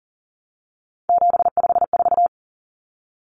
Morse code, heard as a keyed tone, about 40 wpm, 700 Hz. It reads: 754
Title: Common Cut Numbers - 3-Digit 40wpm